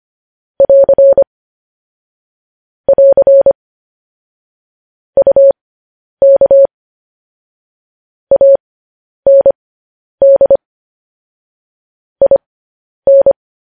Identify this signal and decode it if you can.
Morse code, heard as a keyed tone, about 25 wpm, spaced as for 10 wpm, 550 Hz. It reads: <AR> <AR> UK AND IN
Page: news-headlines-in-morse-code-at-10-wpm